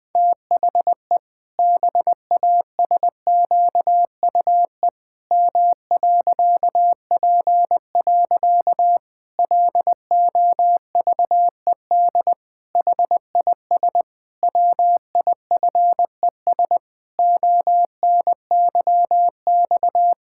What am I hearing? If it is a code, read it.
Morse code, heard as a keyed tone, about 21 wpm, 700 Hz. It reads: THE BASQUE M.P. LOVED HIS WIFES ONYX
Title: Pangram Challenge v2 - 20wpm